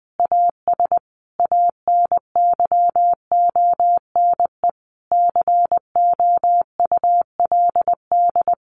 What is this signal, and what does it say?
Morse code, heard as a keyed tone, about 20 wpm, 700 Hz. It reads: AS ANYONE COULD
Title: Sentences from Top 1200 Words 20wpm